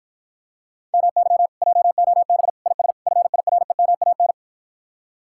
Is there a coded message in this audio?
Yes: Morse code from 0.9 to 4.3 s, heard as a keyed tone, about 54 wpm, 700 Hz. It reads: MY JOB IS FIREMAN